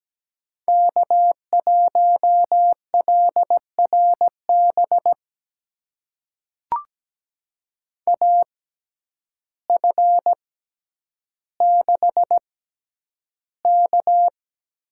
Morse code, heard as a keyed tone, about 17 wpm, spaced as for 9 wpm, 700 Hz. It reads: K1LRB AF6K
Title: USA Call Signs - 5 Characters - 6x Character Spacing - 17wpm